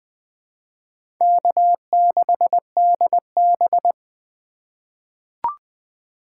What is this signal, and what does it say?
Morse code, heard as a keyed tone, about 21 wpm, 700 Hz. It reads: K6DB
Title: USA Call Signs - 4 Characters - 2x Character Spacing - 20wpm